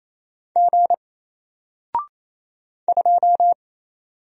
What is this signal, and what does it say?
Morse code, heard as a keyed tone, about 27 wpm, 700 Hz. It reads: G 2
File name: Single-Letter-Number-28wpm